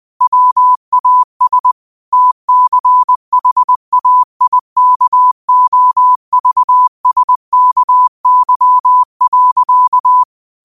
Morse code, heard as a keyed tone, about 20 wpm, 1000 Hz. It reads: WAS TCHAIKOVSKY.
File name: qotdfiles / 20wpm